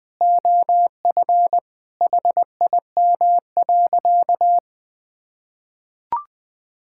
Morse code, heard as a keyed tone, about 20 wpm, 700 Hz. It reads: OF HIM.
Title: Sentences from Top 1600 Words 20wpm